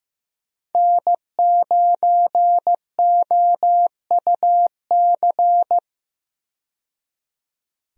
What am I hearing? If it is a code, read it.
Morse code, heard as a keyed tone, about 15 wpm, 700 Hz. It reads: N9OUC
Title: USA Call Signs - 5 Characters - 15wpm